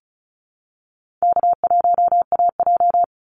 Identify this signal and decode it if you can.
Morse code, heard as a keyed tone, about 34 wpm, 700 Hz. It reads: K1AJ
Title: Easy Call Signs Repeated 2x - 35wpm